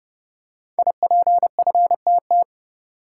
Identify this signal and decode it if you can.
Morse code, heard as a keyed tone, about 30 wpm, 700 Hz. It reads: IPFTT
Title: Code Groups - 5 Letters - 30wpm